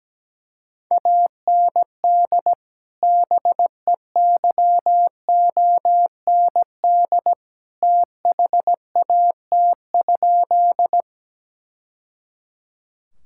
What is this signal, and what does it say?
Morse code, heard as a keyed tone, about 17 wpm, 700 Hz. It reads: AND BEYOND THAT?
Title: Sentences from Top 1000 Words - Review Part 6 - 17wpm